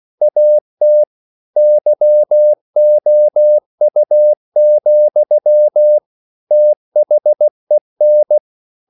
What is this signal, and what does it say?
Morse code, heard as a keyed tone, about 16 wpm, 600 Hz. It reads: AT YOU, THEN
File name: CW / Practice / EnglishQuotations0216